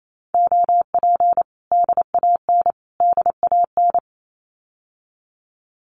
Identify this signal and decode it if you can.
Morse code, heard as a keyed tone, about 28 wpm, 700 Hz. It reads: OP DAN DAN
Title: Common QSO Phrases Encore 28wpm